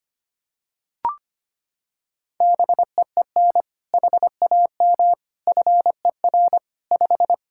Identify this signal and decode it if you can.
Morse code, heard as a keyed tone, about 25 wpm, 700 Hz. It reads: BEEN HAM FER 5
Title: Practice QSO - Exchange 3 - More Info - 25wpm